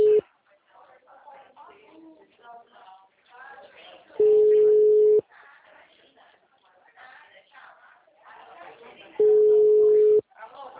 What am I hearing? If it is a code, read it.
Ringback tone (Europe, 425 Hz), with speech.
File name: out- -2021- - - .88866